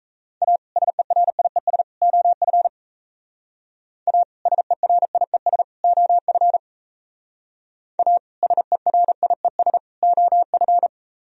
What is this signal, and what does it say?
Morse code, heard as a keyed tone, about 34 wpm, 700 Hz. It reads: A SERIES OF A SERIES OF A SERIES OF
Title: Speed-Racing - Sets of 3 Words 28wpm